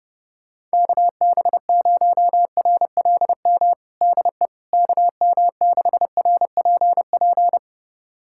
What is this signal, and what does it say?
Morse code, heard as a keyed tone, about 30 wpm, 700 Hz. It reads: KB0RLM DE KM6RPP
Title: USA Callsign DE Callsign - 6 Characters - 30wpm